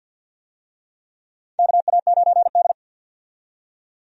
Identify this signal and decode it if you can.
Morse code, heard as a keyed tone, about 52 wpm, 700 Hz. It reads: KA9D
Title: USA Call Signs - 4 Characters - 50wpm